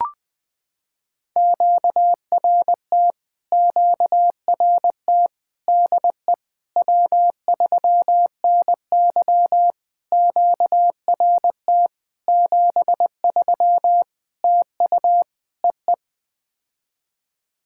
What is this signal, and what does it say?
Morse code, heard as a keyed tone, about 20 wpm, 700 Hz. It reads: QRT QRT DE W3NY QRT 73 TU EE
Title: POTA QSO Practice - 20wpm